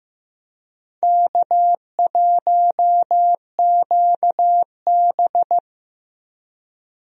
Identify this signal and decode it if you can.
Morse code, heard as a keyed tone, about 15 wpm, 700 Hz. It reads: K1QB
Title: USA Call Signs - 4 Characters - Repeated 2x - 15wpm